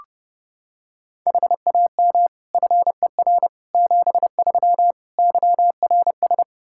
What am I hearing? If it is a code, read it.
Morse code, heard as a keyed tone, about 31 wpm, 700 Hz. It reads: HAM FER 73 YRS
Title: Practice QSO - Exchange 3 - More Info - 30wpm